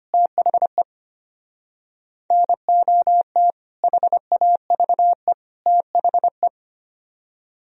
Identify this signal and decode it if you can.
Morse code, heard as a keyed tone, about 31 wpm, 700 Hz. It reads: THE NOT HAVE THE
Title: Speed-Racing - Sets of 3 Words 25wpm